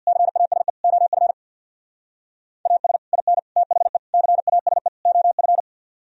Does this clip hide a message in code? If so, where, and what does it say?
Yes: Morse code from 0.1 to 5.6 s, heard as a keyed tone, about 64 wpm, 700 Hz. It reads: CASE OF AS IN THE CASE OF